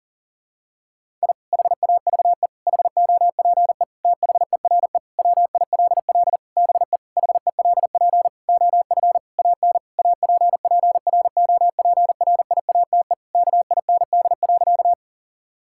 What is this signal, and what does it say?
Morse code, heard as a keyed tone, about 40 wpm, 700 Hz. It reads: I HAVE HOPE THERE WILL BE HELP OF AN APPROPRIATE KIND.
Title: Sentences from Top 1000 Words - Review Part 5 - 40wpm